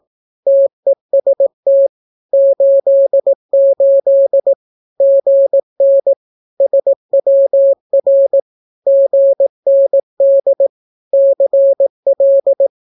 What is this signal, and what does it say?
Morse code, heard as a keyed tone, about 18 wpm, 550 Hz. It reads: TEST 88 GN SWR GND CL
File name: naukapodstaw / skroty / skroty